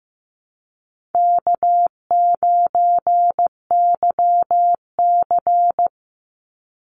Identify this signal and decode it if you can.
Morse code, heard as a keyed tone, about 15 wpm, 700 Hz. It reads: K9YC
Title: Easy Call Signs Repeated 2x - 15wpm